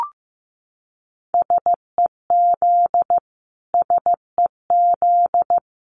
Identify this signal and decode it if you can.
Morse code, heard as a keyed tone, about 15 wpm, 700 Hz. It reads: SEZ SEZ
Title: Abbreviations Repeated 3x - 15wpm